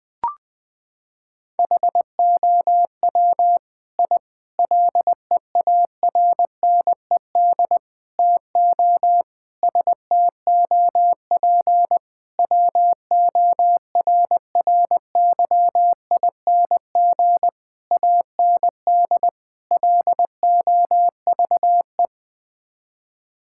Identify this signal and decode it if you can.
Morse code, heard as a keyed tone, about 21 wpm, 700 Hz. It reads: HOW I LEARNED TO STOP WORRYING AND LOVE
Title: Sets of 8 Words - 20wpm